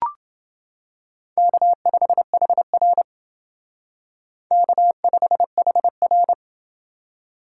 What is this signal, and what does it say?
Morse code, heard as a keyed tone, about 29 wpm, 700 Hz. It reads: K5HR K5HR